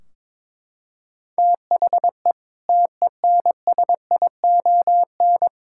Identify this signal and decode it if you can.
Morse code, heard as a keyed tone, about 22 wpm, 700 Hz. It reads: THE TENSION
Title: Sentences from Top 1600 Words 22wpm